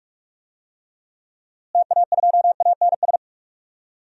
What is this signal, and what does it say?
Morse code, heard as a keyed tone, about 46 wpm, 700 Hz. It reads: TA2ANS
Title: DXCC-100 Call Signs - 6 Characters - Repeated 3x - 45wpm